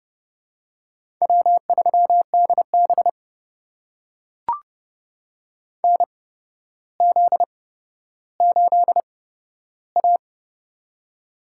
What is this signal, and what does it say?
Morse code, heard as a keyed tone, about 30 wpm, spaced as for 13 wpm, 700 Hz. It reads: W3DB NZ8A
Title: USA Call Signs - 8x Character Spacing - 30wpm